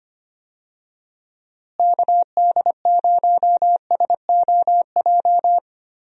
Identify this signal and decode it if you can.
Morse code, heard as a keyed tone, about 25 wpm, 700 Hz. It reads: KD0SOJ
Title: USA Call Signs - 6 Characters - Repeated 2x - 25wpm